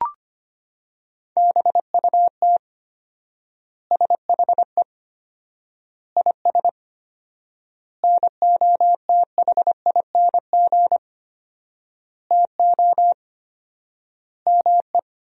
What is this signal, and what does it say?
Morse code, heard as a keyed tone, about 25 wpm, 700 Hz. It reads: BUT SHE IS NOTHING TO ME
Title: 4x Word-Spacing - Sentences from Top 500 Words - 25wpm